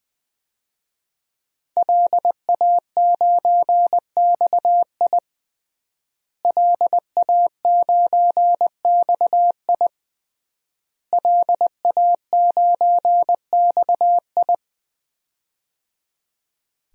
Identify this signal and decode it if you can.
Morse code, heard as a keyed tone, about 21 wpm, 700 Hz. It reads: LA9XI LA9XI LA9XI
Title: DXCC-100 Call Signs - 5 Characters - Repeated 3x - 20wpm